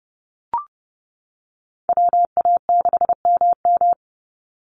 Morse code, heard as a keyed tone, about 30 wpm, 700 Hz. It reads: WA6MM
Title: NA SOTA Chaser Callsigns 30wpm